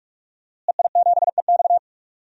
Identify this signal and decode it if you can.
Morse code, heard as a keyed tone, about 46 wpm, 700 Hz. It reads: EI7EX
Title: DXCC-100 Call Signs - 5 Characters - Repeated 3x - 45wpm